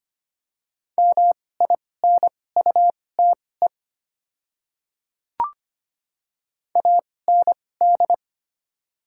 Morse code, heard as a keyed tone, about 25 wpm, spaced as for 18 wpm, 700 Hz. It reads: MINUTE AND
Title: 2x Character Spacing - Top 500 Words - 25wpm